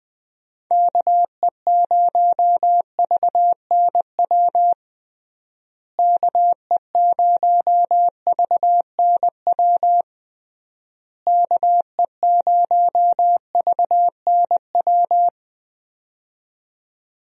Morse code, heard as a keyed tone, about 20 wpm, 700 Hz. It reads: KE0VNW KE0VNW KE0VNW
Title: USA Call Signs - 6 Characters - Repeated 3x - 20wpm